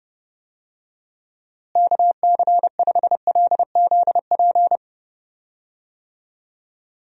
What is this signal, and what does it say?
Morse code, heard as a keyed tone, about 30 wpm, 700 Hz. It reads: KC5LZP
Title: USA Call Signs - 6 Characters - Repeated 2x - 30wpm